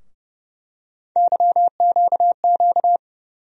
Morse code, heard as a keyed tone, about 29 wpm, 700 Hz. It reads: YQQ
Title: USA Call Sign Postfixes - 3 Letters 30wpm